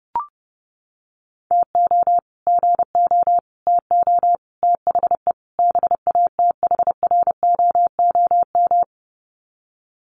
Morse code, heard as a keyed tone, about 30 wpm, 700 Hz. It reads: TO GO TO THE BATHROOM
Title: Sets of 5 Words Encore 30wpm